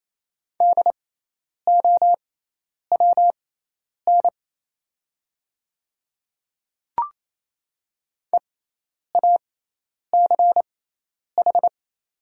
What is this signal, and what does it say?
Morse code, heard as a keyed tone, about 27 wpm, spaced as for 10 wpm, 700 Hz. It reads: DOWN EACH
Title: 6x Character Spacing - Top 100 Words - 28wpm